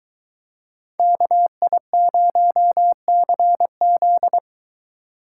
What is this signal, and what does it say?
Morse code, heard as a keyed tone, about 23 wpm, 700 Hz. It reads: KI0CZ